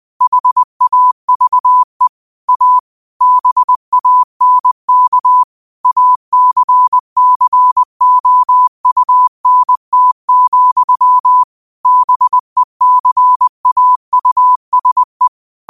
Morse code, heard as a keyed tone, about 20 wpm, 1000 Hz. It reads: HAVE A BANK ACCOUNT, BECAUSE